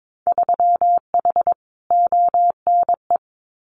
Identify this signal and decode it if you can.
Morse code, heard as a keyed tone, about 22 wpm, 700 Hz. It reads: 3H ONE
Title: Winter Field Day Exchanges Encore 22wpm